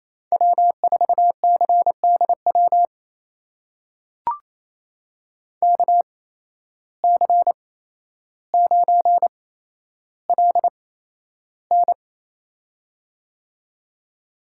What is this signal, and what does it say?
Morse code, heard as a keyed tone, about 28 wpm, spaced as for 12 wpm, 700 Hz. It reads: W4CDW KC9LN
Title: USA Call Signs - 5 Characters - 8x Character Spacing - 28wpm